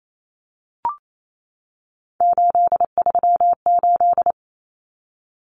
Morse code, heard as a keyed tone, about 28 wpm, 700 Hz. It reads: 838
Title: Numbers - 3-Digit 28wpm